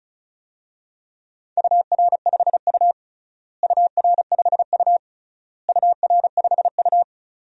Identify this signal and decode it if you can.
Morse code, heard as a keyed tone, about 34 wpm, 700 Hz. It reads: UR5U UR5U UR5U
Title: DXCC-100 Call Signs - 4 Characters - Repeated 3x - 35wpm